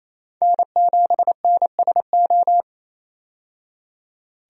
Morse code, heard as a keyed tone, about 28 wpm, 700 Hz. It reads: N7NSO
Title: USA Call Signs - 5 Characters - 28wpm